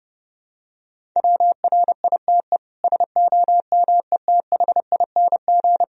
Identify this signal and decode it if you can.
Morse code, heard as a keyed tone, about 29 wpm, 700 Hz. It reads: WRITE SOMETHING
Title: Sentences from Top 1000 Words - Review Part 1 - 30wpm